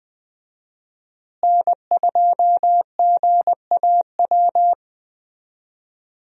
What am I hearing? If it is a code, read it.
Morse code, heard as a keyed tone, about 19 wpm, 700 Hz. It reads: N2GAW
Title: USA Call Signs - 5 Characters - Repeated 3x - 20wpm